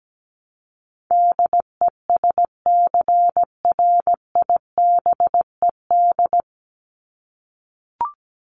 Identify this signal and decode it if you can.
Morse code, heard as a keyed tone, about 17 wpm, 700 Hz. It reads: DESCRIBED
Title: ED Suffix - Top 100 Words with Base Words 17wpm